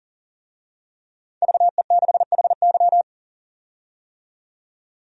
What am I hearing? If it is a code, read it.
Morse code, heard as a keyed tone, about 37 wpm, 700 Hz. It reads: VE6HY
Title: DXCC-100 Call Signs - 5 Characters - 40wpm